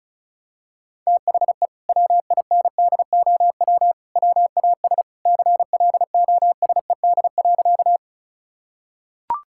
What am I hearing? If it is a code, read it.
Morse code, heard as a keyed tone, about 36 wpm, 700 Hz. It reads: THE WINDOW WAS CLOSED.
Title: Sentences from Top 1600 Words 35wpm